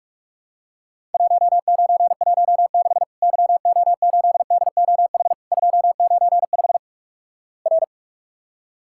Morse code, heard as a keyed tone, about 46 wpm, 700 Hz. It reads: 1916 YO8DOH 295 R
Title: Worked All Europe DX Contest - QTC Exchange - 45wpm